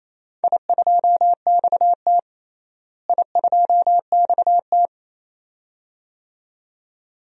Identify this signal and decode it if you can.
Morse code, heard as a keyed tone, about 28 wpm, 700 Hz. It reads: I2XT I2XT
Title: DXCC-100 Call Signs - 4 Characters - Repeated 2x - 28wpm